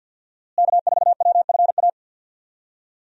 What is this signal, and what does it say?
Morse code, heard as a keyed tone, about 47 wpm, 700 Hz. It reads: K4WFA
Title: USA Call Signs - 5 Characters - Repeated 2x - 50wpm